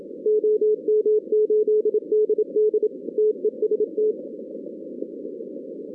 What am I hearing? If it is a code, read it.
Morse code, heard as a keyed tone, about 26 wpm, 440 Hz. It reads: OM8DD TEST
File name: 2016 arrl dx cw / 2016-ARRL-DX-CW-K5TR- - 0-41